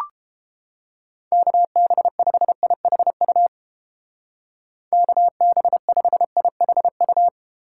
Morse code, heard as a keyed tone, about 31 wpm, 700 Hz. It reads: KB5IHU KB5IHU